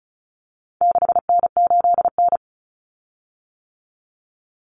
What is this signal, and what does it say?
Morse code, heard as a keyed tone, about 34 wpm, 700 Hz. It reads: 6N8N
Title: Common Cut Numbers - 4-Digit 35wpm